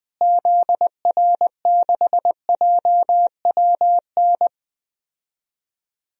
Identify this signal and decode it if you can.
Morse code, heard as a keyed tone, about 21 wpm, 700 Hz. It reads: ZR6JWN
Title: DXCC-100 Call Signs - 6 Characters - Repeated 2x - 20wpm